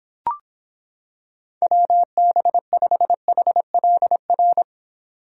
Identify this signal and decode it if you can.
Morse code, heard as a keyed tone, about 26 wpm, 700 Hz. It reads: WB5HLR